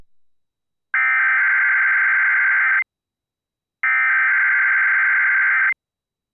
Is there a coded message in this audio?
Yes: Emergency Alert System (SAME) tones (header bursts) from 0.9 to 5.6 s, heard as harsh buzzing bursts.